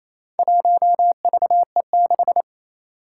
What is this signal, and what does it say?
Morse code, heard as a keyed tone, about 28 wpm, 700 Hz. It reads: 1VE6
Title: Four Challenging Letter-Number Combinations 28wpm